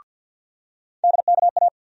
Morse code, heard as a keyed tone, about 48 wpm, 700 Hz. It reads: DKA
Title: USA Call Sign Postfixes - 3 Letters 50wpm